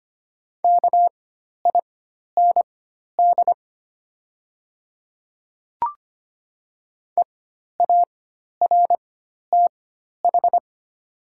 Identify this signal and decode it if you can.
Morse code, heard as a keyed tone, about 25 wpm, spaced as for 12 wpm, 690 Hz. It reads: KIND EARTH
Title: 4x Character Spacing - Top 500 Words - 25wpm